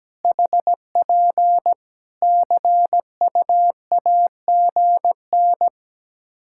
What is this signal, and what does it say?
Morse code, heard as a keyed tone, about 17 wpm, 700 Hz. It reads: HP CUAGN
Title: Practice QSO - Exchange 4 - Ending - 17wpm